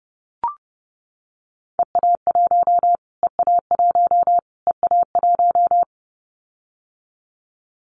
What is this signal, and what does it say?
Morse code, heard as a keyed tone, about 30 wpm, 690 Hz. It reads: EA1 EA1 EA1
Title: Callsign Prefixes for Top 25 DXCC - Repeated 3x - 30wpm